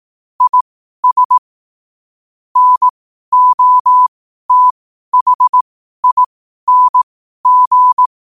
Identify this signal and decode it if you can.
Morse code, heard as a keyed tone, about 18 wpm, spaced as for 12 wpm, 1000 Hz. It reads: IS NOTHING
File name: qotdfiles / 12wpm